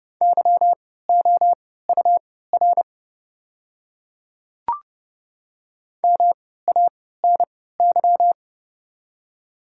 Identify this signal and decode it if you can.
Morse code, heard as a keyed tone, about 30 wpm, spaced as for 17 wpm, 700 Hz. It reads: YOUR MANY
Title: 3x Character Spacing - Top 200 Words - 30wpm